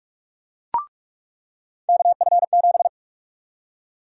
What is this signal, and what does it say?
Morse code, heard as a keyed tone, about 46 wpm, 700 Hz. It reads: KF7
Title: USA Call Sign Prefixes - 45wpm